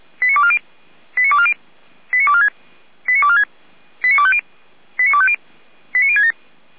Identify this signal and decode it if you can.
Five-tone selective call (ZVEI), heard as quick runs of five tones, tones of about 70 ms. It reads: 89240 89240 89246 89246 89250 89250 89067
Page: audio.f85360.feuerwehr-orlamuende.org